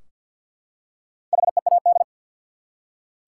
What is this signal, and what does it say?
Morse code, heard as a keyed tone, about 53 wpm, 700 Hz. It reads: HEAD
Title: Four Letter Words - 50wpm